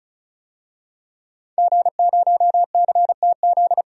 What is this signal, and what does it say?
Morse code, heard as a keyed tone, about 34 wpm, 700 Hz. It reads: G0CTZ
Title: DXCC-100 Call Signs - 5 Characters - 35wpm